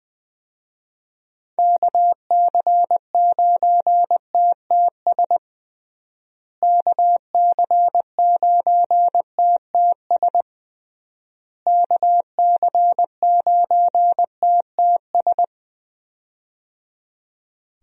Morse code, heard as a keyed tone, about 20 wpm, 700 Hz. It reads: KC9TTS KC9TTS KC9TTS
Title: USA Call Signs - 6 Characters - Repeated 3x - 20wpm